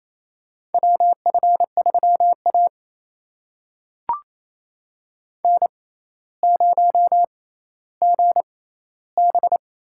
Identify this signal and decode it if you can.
Morse code, heard as a keyed tone, about 28 wpm, spaced as for 14 wpm, 700 Hz. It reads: WF3A N0GB
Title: USA Call Signs - 6x Character Spacing - 28wpm